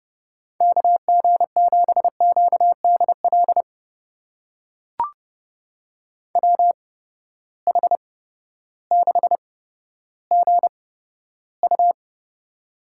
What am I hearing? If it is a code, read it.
Morse code, heard as a keyed tone, about 30 wpm, spaced as for 14 wpm, 700 Hz. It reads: KG7QDL WH6GU
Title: USA Call Signs - 8x Character Spacing - 30wpm